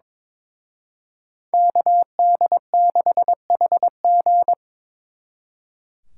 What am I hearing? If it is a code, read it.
Morse code, heard as a keyed tone, about 25 wpm, 700 Hz. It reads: KD6HG
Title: Speed Racing - USA 2x2 Call Signs 22wpm